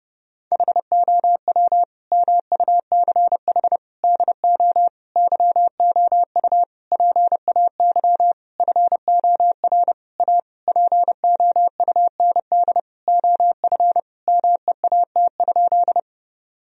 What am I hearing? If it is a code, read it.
Morse code, heard as a keyed tone, about 32 wpm, 700 Hz. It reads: HOW MUCH DO YOU PAY FOR A POUND OF MEAT?
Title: Sentences from Top 1000 Words - Review Part 3 - 30wpm